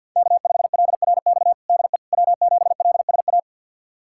Morse code, heard as a keyed tone, about 53 wpm, 700 Hz. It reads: K5LRX DE W7LSA
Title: USA Callsign DE Callsign 50wpm